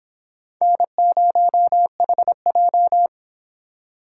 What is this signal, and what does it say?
Morse code, heard as a keyed tone, about 23 wpm, 700 Hz. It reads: N0HJ